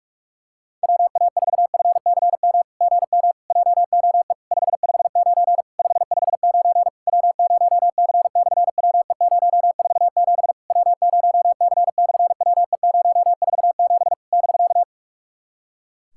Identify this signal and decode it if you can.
Morse code, heard as a keyed tone, about 46 wpm, 700 Hz. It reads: WA4FCM GM JOE 559 559 W0C/WE047 W0C/WE047 <BK>
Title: SOTA QSO Practice 45wpm